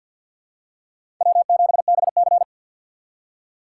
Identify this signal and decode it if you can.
Morse code, heard as a keyed tone, about 53 wpm, 700 Hz. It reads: W7BC
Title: USA Call Signs - 4 Characters - 6x Character Spacing - 50wpm